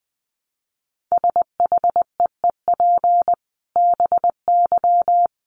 Morse code, heard as a keyed tone, about 20 wpm, 700 Hz. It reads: SHEEP BY
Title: Dit Challenge 20wpm